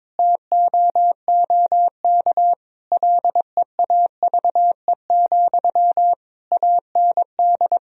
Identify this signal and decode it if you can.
Morse code, heard as a keyed tone, about 22 wpm, 700 Hz. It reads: TOOK LEAVE, AND
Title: Sentences from Top 1000 Words - Review Part 1 - 22wpm